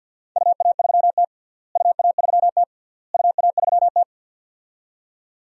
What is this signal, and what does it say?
Morse code, heard as a keyed tone, about 51 wpm, 700 Hz. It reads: UA3T UA3T UA3T
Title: DXCC-100 Call Signs - 4 Characters - Repeated 3x - 50wpm